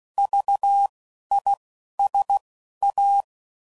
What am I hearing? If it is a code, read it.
Morse code, heard as a keyed tone, about 16 wpm, 800 Hz. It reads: V I S A
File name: Prefixes / By First Letter / Vs